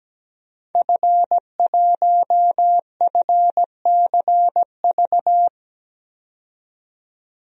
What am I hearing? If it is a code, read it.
Morse code, heard as a keyed tone, about 17 wpm, 700 Hz. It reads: F1FCV
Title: DXCC-100 Call Signs - 5 Characters - 17wpm